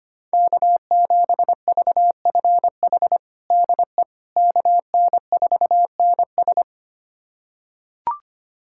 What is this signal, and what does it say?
Morse code, heard as a keyed tone, about 25 wpm, 700 Hz. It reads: K7VFH DE KN4NS
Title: USA Callsign DE Callsign - 5 Characters - 25wpm